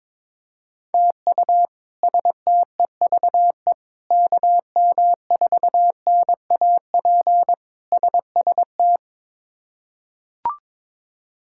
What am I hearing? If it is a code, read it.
Morse code, heard as a keyed tone, about 23 wpm, 700 Hz. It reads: TU STEVE KM4NAP SST
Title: K1USN SST QSO Practice - 22wpm